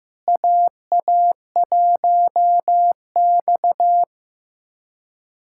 Morse code, heard as a keyed tone, about 15 wpm, 700 Hz. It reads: AA1X
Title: USA Callsign DE Callsign - 4 Characters - 15wpm